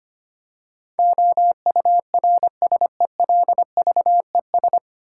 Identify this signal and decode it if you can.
Morse code, heard as a keyed tone, about 25 wpm, 700 Hz. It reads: OURSELVES
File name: Nine Letter Words - 25wpm